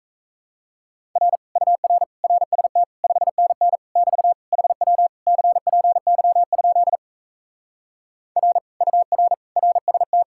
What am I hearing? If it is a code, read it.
Morse code, heard as a keyed tone, about 38 wpm, 700 Hz. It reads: R UR RST 5NN <BT> HW CPY? R UR RST